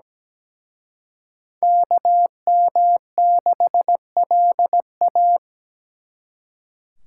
Morse code, heard as a keyed tone, about 20 wpm, 700 Hz. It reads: KM6LA
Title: Speed Racing - USA 2x2 Call Signs 17wpm